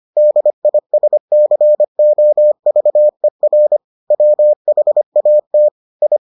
Morse code, heard as a keyed tone, about 25 wpm, 600 Hz. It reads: DISCOVER WHAT I